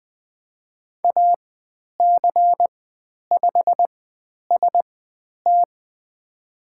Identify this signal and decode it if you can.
Morse code, heard as a keyed tone, about 20 wpm, spaced as for 10 wpm, 700 Hz. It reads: AC5ST
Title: 102 Lesson 17g - Letter u - Call Signs 20wpm